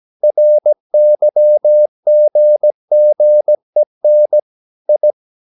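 Morse code, heard as a keyed tone, about 17 wpm, 600 Hz. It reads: RYGGEN I
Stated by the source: Slow (QRS) Morse practice at 17 wpm, from CW med Gnister.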